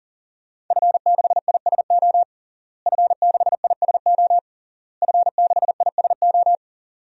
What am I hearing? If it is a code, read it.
Morse code, heard as a keyed tone, about 40 wpm, 700 Hz. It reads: F6ISO F6ISO F6ISO
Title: DXCC-100 Call Signs - 5 Characters - Repeated 3x - 40wpm